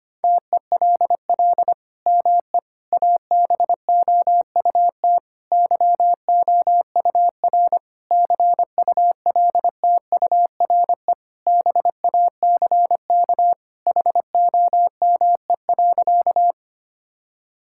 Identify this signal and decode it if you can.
Morse code, heard as a keyed tone, about 25 wpm, 700 Hz. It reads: TELL ME ABOUT YOUR CULTURE BACK HOME.
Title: Sentences from Top 1000 Words - Review Part 6 - 25wpm